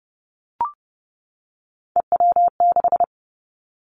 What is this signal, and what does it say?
Morse code, heard as a keyed tone, about 30 wpm, 700 Hz. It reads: EW6
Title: Callsign Prefixes for Top 50 DXCC - 30wpm